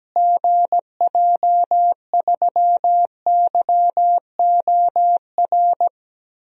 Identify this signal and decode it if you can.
Morse code, heard as a keyed tone, about 17 wpm, 700 Hz. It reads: GJ3YOR
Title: DXCC-100 Call Signs - 6 Characters - Repeated 3x - 17wpm